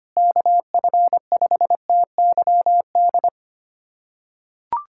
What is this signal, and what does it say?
Morse code, heard as a keyed tone, about 25 wpm, 700 Hz. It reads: KF5TYD
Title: USA Callsign DE Callsign - 6 Characters - 25wpm